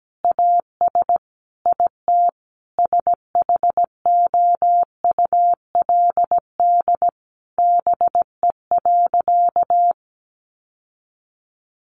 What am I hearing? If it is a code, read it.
Morse code, heard as a keyed tone, about 17 wpm, 700 Hz. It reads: AS IT SHOULD BE.
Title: Sentences from Top 900 Words 17wpm